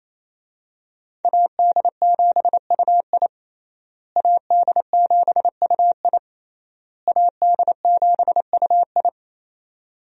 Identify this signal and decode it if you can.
Morse code, heard as a keyed tone, about 28 wpm, 700 Hz. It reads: AD7UI AD7UI AD7UI
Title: USA Call Signs - 5 Characters - Repeated 3x - 28wpm